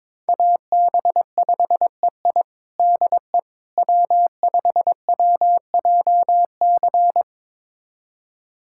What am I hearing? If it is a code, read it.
Morse code, heard as a keyed tone, about 22 wpm, 700 Hz. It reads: AB5EI DE W5WJC
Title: USA Callsign DE Callsign - 5 Characters - 22wpm